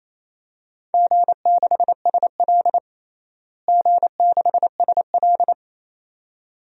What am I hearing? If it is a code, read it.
Morse code, heard as a keyed tone, about 28 wpm, 700 Hz. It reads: G6SL G6SL
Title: DXCC-100 Call Signs - 4 Characters - Repeated 2x - 28wpm